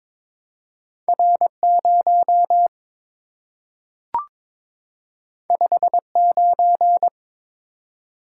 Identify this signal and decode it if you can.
Morse code, heard as a keyed tone, about 22 wpm, 700 Hz. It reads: R0 59
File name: practice / 22wpm / Two Letter-Number Combinations - 22wpm